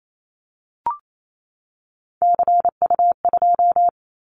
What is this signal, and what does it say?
Morse code, heard as a keyed tone, about 28 wpm, 700 Hz. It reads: CU2
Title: Callsign Prefixes for Top 50 DXCC - 28wpm